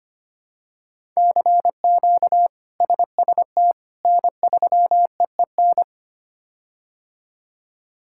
Morse code, heard as a keyed tone, about 25 wpm, 700 Hz. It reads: CQ SST N3EEN
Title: K1USN SST QSO Practice - 25wpm